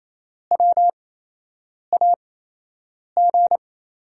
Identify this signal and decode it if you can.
Morse code, heard as a keyed tone, about 28 wpm, spaced as for 8 wpm, 700 Hz. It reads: WAG
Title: USA Call Signs - 6 Characters - 8x Character Spacing - 28wpm